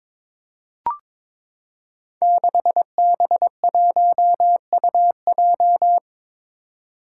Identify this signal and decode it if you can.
Morse code, heard as a keyed tone, about 22 wpm, 700 Hz. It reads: 6B1UJ
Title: Five Challenging Letter-Number Combinations 22wpm